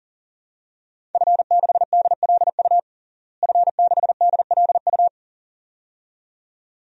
Morse code, heard as a keyed tone, about 40 wpm, 700 Hz. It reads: F6DLU F6DLU
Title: DXCC-100 Call Signs - 5 Characters - Repeated 2x - 40wpm